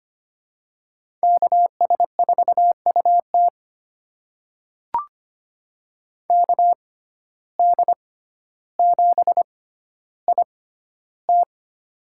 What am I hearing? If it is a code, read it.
Morse code, heard as a keyed tone, about 25 wpm, spaced as for 13 wpm, 700 Hz. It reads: KS4UT KD7IT
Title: USA Call Signs - 5 Characters - 6x Character Spacing - 25wpm